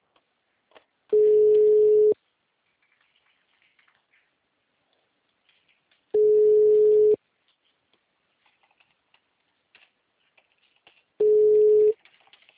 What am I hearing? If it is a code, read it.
Ringback tone (Europe, 425 Hz).